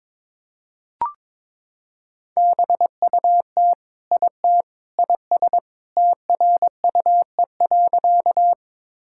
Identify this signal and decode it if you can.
Morse code, heard as a keyed tone, about 22 wpm, 700 Hz. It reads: BUT IT IS TRUE.
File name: practice / 22wpm / Sentences with 3-Word Phrases - 22wpm